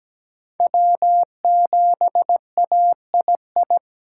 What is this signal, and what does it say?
Morse code, heard as a keyed tone, about 20 wpm, 700 Hz. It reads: W7AII
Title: Speed Racing - USA Call Signs 17wpm